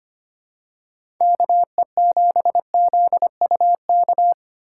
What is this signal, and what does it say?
Morse code, heard as a keyed tone, about 25 wpm, 700 Hz. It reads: KE7ZUK
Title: USA Call Signs - 3x Character Spacing - 25wpm